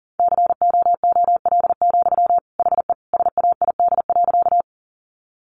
Morse code, heard as a keyed tone, about 40 wpm, 700 Hz. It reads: COOL, HE SAID.
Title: Sentences from Top 1200 Words 40wpm